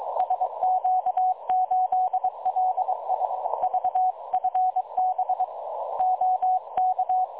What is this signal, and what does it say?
Morse code, heard as a keyed tone, about 22 wpm, 740 Hz. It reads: SQ8LI VFB OK
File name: dlazaawansowanych / nagraniazpasma / websdr recording 3565.0kHz